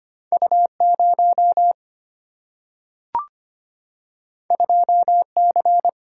Morse code, heard as a keyed tone, about 25 wpm, 700 Hz. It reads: U 0 2 C
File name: Two Letter-Number Combinations - 25wpm